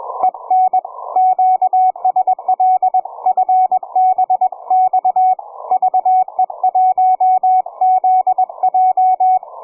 Morse code, heard as a keyed tone, about 24 wpm, 750 Hz, with radio noise. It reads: EN QSL FBX VE1ZJ
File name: cw forum / ve1zj 1830 160m 04h00